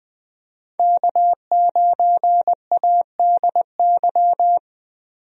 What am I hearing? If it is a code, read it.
Morse code, heard as a keyed tone, about 21 wpm, 700 Hz. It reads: K9ADY
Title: USA Call Signs - 5 Characters - Repeated 3x - 20wpm